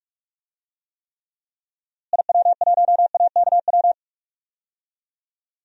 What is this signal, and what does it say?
Morse code, heard as a keyed tone, about 44 wpm, 700 Hz. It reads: IW1AKW
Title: DXCC-100 Call Signs - 6 Characters - Repeated 3x - 45wpm